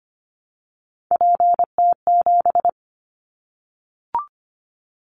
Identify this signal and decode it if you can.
Morse code, heard as a keyed tone, about 25 wpm, 700 Hz. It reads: PT7
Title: Callsign Prefixes for Top 100 DXCC - 25wpm